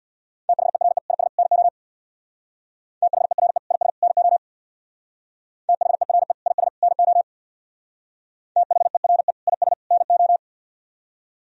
Morse code, heard as a keyed tone, about 64 wpm, 700 Hz. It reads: THERE IS NO THERE IS NO THERE IS NO THERE IS NO